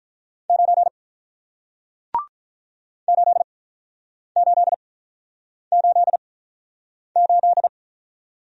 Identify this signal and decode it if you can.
Morse code, heard as a keyed tone, about 48 wpm, spaced as for 9 wpm, 700 Hz. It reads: 9 8888